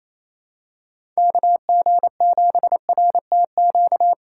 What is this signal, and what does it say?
Morse code, heard as a keyed tone, about 28 wpm, 700 Hz. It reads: KG7RTQ
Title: USA Call Signs - 6 Characters - 4x Character Spacing - 28wpm